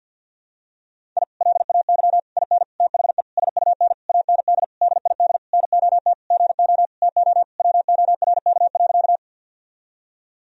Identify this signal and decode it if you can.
Morse code, heard as a keyed tone, about 46 wpm, 700 Hz. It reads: I LAY IN THE SUN AND DID NOT GO TO WORK.
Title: Sentences from Top 1000 Words - Review Part 2 - 50wpm